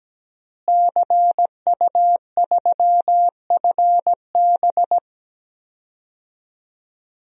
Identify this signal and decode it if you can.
Morse code, heard as a keyed tone, about 17 wpm, 700 Hz. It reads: CU3FB
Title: DXCC-100 Call Signs - 5 Characters - Repeated 2x - 17wpm